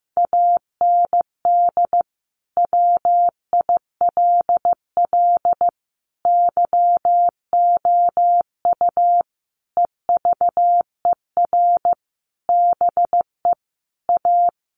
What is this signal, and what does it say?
Morse code, heard as a keyed tone, about 15 wpm, 700 Hz. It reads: AND WILL YOU EVER BE A
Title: Sentences from Top 1500 Words 15wpm